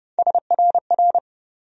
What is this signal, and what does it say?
Morse code, heard as a keyed tone, about 30 wpm, 700 Hz. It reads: SRR
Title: USA Call Sign Postfixes - 3 Letters 30wpm